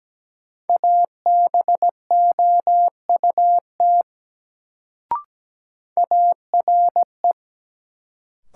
Morse code, heard as a keyed tone, about 17 wpm, 700 Hz. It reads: ABOUT ARE
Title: Sets of 2 Words - New Word Drill - 17wpm